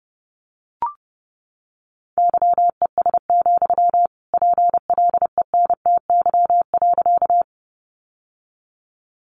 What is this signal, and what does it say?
Morse code, heard as a keyed tone, about 30 wpm, 700 Hz. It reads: YES, PLENTY.
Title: Sentences from Top 1300 Words 30wpm